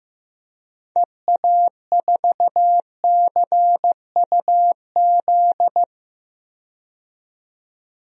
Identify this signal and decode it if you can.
Morse code, heard as a keyed tone, about 15 wpm, 700 Hz. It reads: EA4CUZ
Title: DXCC-100 Call Signs - 6 Characters - 15wpm